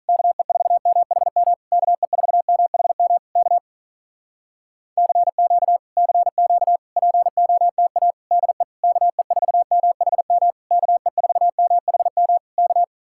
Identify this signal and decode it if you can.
Morse code, heard as a keyed tone, about 48 wpm, 700 Hz. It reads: KE4MHM KE4MHM K CQ CQ POTA DE KE4MHM KE4MHM K